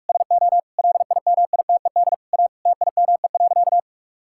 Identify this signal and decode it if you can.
Morse code, heard as a keyed tone, about 46 wpm, 700 Hz. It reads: SO LIMITED A TIME.
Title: Sentences from Top 1000 Words - Review Part 6 - 45wpm